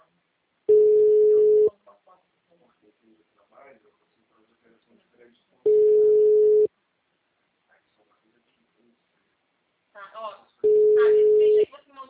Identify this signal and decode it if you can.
Ringback tone (Europe, 425 Hz), with speech.